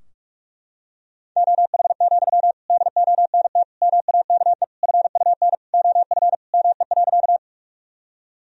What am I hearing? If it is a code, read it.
Morse code, heard as a keyed tone, about 42 wpm, 700 Hz. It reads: OH, DONT MAKE FUN OF ME.
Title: Sentences from Top 1000 Words - Review Part 4 - 45wpm